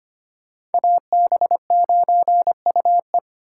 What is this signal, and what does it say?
Morse code, heard as a keyed tone, about 32 wpm, 700 Hz. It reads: AB9UE